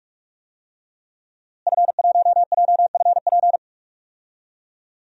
Morse code, heard as a keyed tone, about 43 wpm, 700 Hz. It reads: F1JFP
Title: DXCC-100 Call Signs - 5 Characters - Repeated 2x - 45wpm